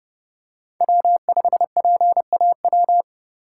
Morse code, heard as a keyed tone, about 32 wpm, 700 Hz. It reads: W5PAW
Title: USA Call Signs - 5 Characters - 2x Character Spacing - 30wpm